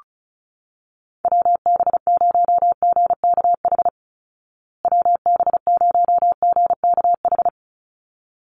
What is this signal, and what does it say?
Morse code, heard as a keyed tone, about 34 wpm, 700 Hz. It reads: WB0GKH WB0GKH
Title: Call Signs Repeated 3x - 35wpm